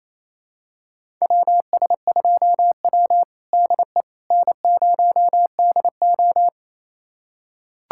Morse code, heard as a keyed tone, about 28 wpm, 700 Hz. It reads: WS2W DE N0DO
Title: USA Callsign DE Callsign - 4 Characters - 28wpm